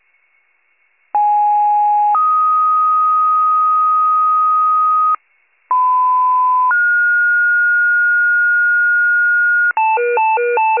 Two-tone page, heard as a one-second tone, then a three-second tone, about 830 then 1250 Hz, about 1010 then 1430 Hz, with noise. It is heard from 1.1 to 9.7 s.